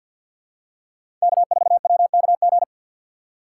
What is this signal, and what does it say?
Morse code, heard as a keyed tone, about 51 wpm, 700 Hz. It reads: K4WKG
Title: USA Call Signs - 5 Characters - 4x Character Spacing - 50wpm